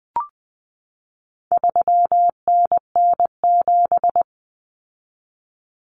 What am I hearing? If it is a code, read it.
Morse code, heard as a keyed tone, about 20 wpm, 700 Hz. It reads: 3NN7
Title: Common Cut Numbers - 4-Digit 20wpm